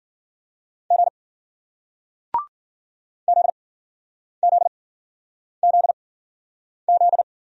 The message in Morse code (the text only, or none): G ZZZZ